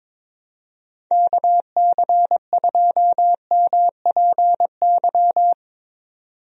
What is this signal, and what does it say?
Morse code, heard as a keyed tone, about 22 wpm, 700 Hz. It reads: KC2MPY
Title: USA Call Signs - 6 Characters - 22wpm